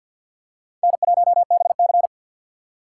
Morse code, heard as a keyed tone, about 51 wpm, 700 Hz. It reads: N1BC
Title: USA Call Signs - 4 Characters - 2x Character Spacing - 50wpm